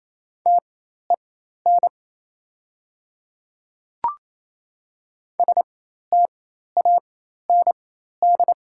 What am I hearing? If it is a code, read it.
Morse code, heard as a keyed tone, about 27 wpm, spaced as for 13 wpm, 700 Hz. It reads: TEN STAND
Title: 4x Character Spacing - Top 500 Words - 28wpm